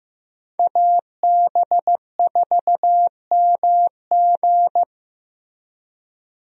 Morse code, heard as a keyed tone, about 15 wpm, 700 Hz. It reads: AB4MG
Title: USA Call Signs - 5 Characters - Repeated 2x - 15wpm